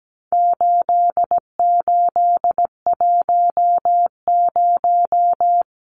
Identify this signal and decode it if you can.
Morse code, heard as a keyed tone, about 17 wpm, 700 Hz. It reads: 8810
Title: Numbers - 4-Digit 17wpm